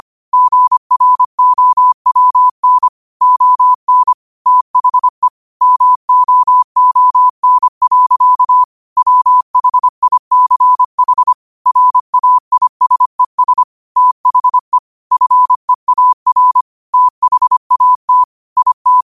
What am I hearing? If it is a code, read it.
Morse code, heard as a keyed tone, about 25 wpm, 1000 Hz. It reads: GROWN ON THE MOON. WHICH RAISES THE FEAR THAT IT